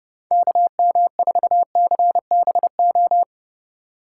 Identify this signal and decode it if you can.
Morse code, heard as a keyed tone, about 29 wpm, 700 Hz. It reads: KM4CBO